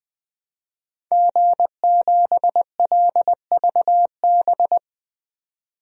Morse code, heard as a keyed tone, about 20 wpm, 700 Hz. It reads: G7LVB
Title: DXCC-100 Call Signs - 5 Characters - 20wpm